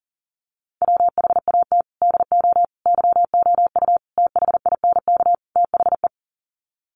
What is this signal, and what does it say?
Morse code, heard as a keyed tone, about 40 wpm, 700 Hz. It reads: WHAT DO YOU THINK THE
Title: Sets of 5 Words Encore 40wpm